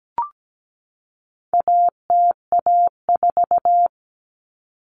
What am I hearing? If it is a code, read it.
Morse code, heard as a keyed tone, about 17 wpm, 700 Hz. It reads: ATA4
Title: Common Cut Numbers - 4-Digit 17wpm